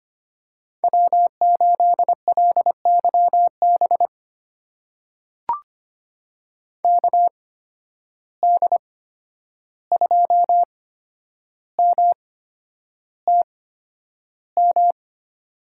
Morse code, heard as a keyed tone, about 25 wpm, spaced as for 10 wpm, 700 Hz. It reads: W8LYB KD2MTM
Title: USA Call Signs - 8x Character Spacing - 25wpm